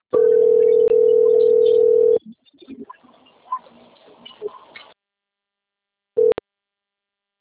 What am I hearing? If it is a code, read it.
Ringback tone (North America), 440 + 480 Hz.